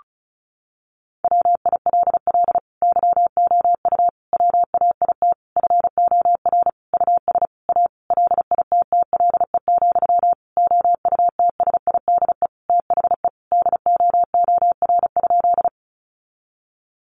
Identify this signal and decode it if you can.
Morse code, heard as a keyed tone, about 34 wpm, 700 Hz. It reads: WILL YOU WAIT FOR US A LITTLE, OUTSIDE THE DOOR?
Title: Sentences from Top 900 Words 35wpm